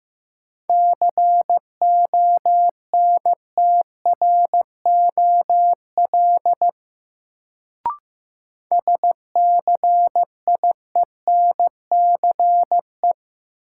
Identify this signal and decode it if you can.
Morse code, heard as a keyed tone, about 16 wpm, 700 Hz. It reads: CONTROL SCIENCE
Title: Sentences from Top 1000 Words - Review Part 2 - New Word Drill - 15wpm